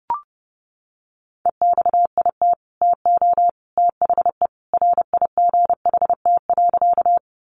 Morse code, heard as a keyed tone, about 30 wpm, 700 Hz. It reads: EXIT TO THE RIGHT.
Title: Sentences from Top 1100 Words 30wpm